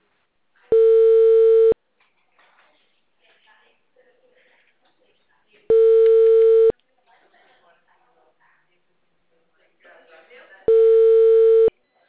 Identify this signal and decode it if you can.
Ringback tone (China, 450 Hz), with speech.